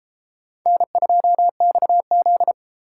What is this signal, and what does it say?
Morse code, heard as a keyed tone, about 32 wpm, 700 Hz. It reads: N2XZ